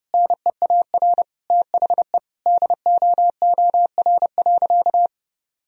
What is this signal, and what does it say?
Morse code, heard as a keyed tone, about 30 wpm, 700 Hz. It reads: NEAR THE DOOR.
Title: Sentences from Top 1000 Words - Review Part 2 - 30wpm